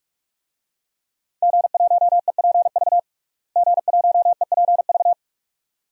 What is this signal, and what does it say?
Morse code, heard as a keyed tone, about 46 wpm, 700 Hz. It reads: G1EPV G1EPV
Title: DXCC-100 Call Signs - 5 Characters - Repeated 2x - 45wpm